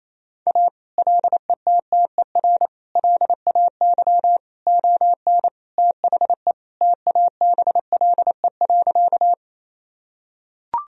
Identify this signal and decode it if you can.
Morse code, heard as a keyed tone, about 28 wpm, 700 Hz. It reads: A LETTER LAY ON THE TABLE.
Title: Sentences from Top 1000 Words - Review Part 2 - 28wpm